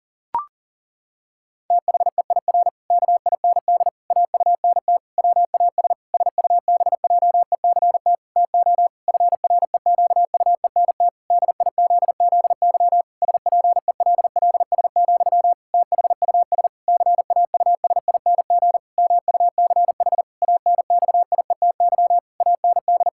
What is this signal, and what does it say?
Morse code, heard as a keyed tone, about 40 wpm, 700 Hz. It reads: THEIR KIND AUNT WAS SUBJECT TO FREQUENT DIZZY SPELLS, THUS CAUSING MUCH ANXIETY AND
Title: Pangram Challenge v2 - 40wpm